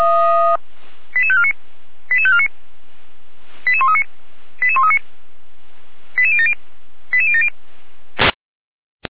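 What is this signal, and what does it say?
Five-tone selective call (ZVEI), heard as quick runs of five tones, tones of about 70 ms. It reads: 88549 88549 88139 88139 88070 88070
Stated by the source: ZVEI five-tone selective call received by SDR, recorded by Feuerwehr Orlamünde fire brigade.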